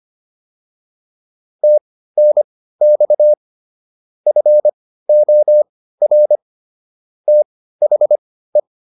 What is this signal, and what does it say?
Morse code, heard as a keyed tone, about 25 wpm, spaced as for 15 wpm, 600 Hz. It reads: TNX FOR THE
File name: PR302 15